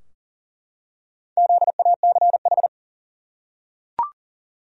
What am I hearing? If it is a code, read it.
Morse code, heard as a keyed tone, about 40 wpm, 700 Hz. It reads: ZACH
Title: USA Common Names - Max 5 Letters - 40wpm